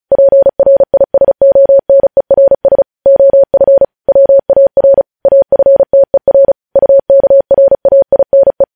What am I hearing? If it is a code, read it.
Morse code, heard as a keyed tone, about 36 wpm, 550 Hz. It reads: PRISONERS OF WAR AFTER UKRAINE
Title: News Headlines at 35